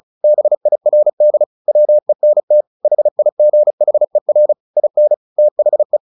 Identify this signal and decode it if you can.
Morse code, heard as a keyed tone, about 34 wpm, 600 Hz. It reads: BIRD WENT HIGHER IN THE
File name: the old man and the sea / the old man and the sea 1 35wpm